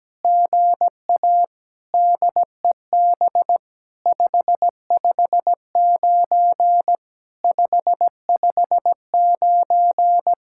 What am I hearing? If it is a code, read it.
Morse code, heard as a keyed tone, about 17 wpm, 700 Hz. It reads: GA DEB 559 559
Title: SOTA QSO Practice 17wpm